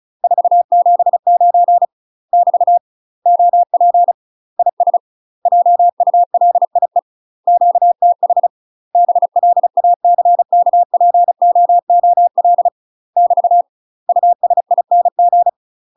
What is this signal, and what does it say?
Morse code, heard as a keyed tone, about 34 wpm, 700 Hz. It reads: 479 <BT> OP IS JULIE QTH BLACKPOOL <BT> USING
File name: UK MASTER QSOs TRAINING / 35WPM / UK MASTER QSOs 0008